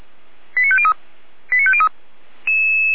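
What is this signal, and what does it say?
Five-tone selective call (ZVEI), heard as quick runs of five tones, tones of about 70 ms. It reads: 89503 89503
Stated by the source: ZVEI five-tone call received on SDR by feuerwehr-orlamuende fire brigade.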